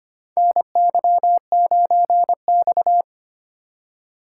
Morse code, heard as a keyed tone, about 25 wpm, 700 Hz. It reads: NY9X
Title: K1USN SST QSO Practice - 25wpm